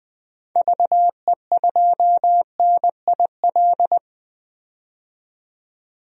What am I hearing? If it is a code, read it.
Morse code, heard as a keyed tone, about 21 wpm, 700 Hz. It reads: VE2NIL
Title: DXCC-100 Call Signs - 6 Characters - 20wpm